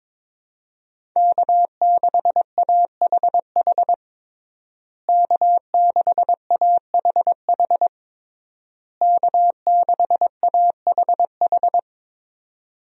Morse code, heard as a keyed tone, about 22 wpm, 700 Hz. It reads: K6AHH K6AHH K6AHH
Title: USA Call Signs - 5 Characters - Repeated 3x - 22wpm